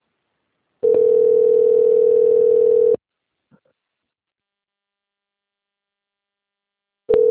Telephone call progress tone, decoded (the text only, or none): ringback tone